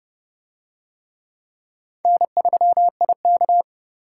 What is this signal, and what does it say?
Morse code, heard as a keyed tone, about 31 wpm, 700 Hz. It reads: N3IK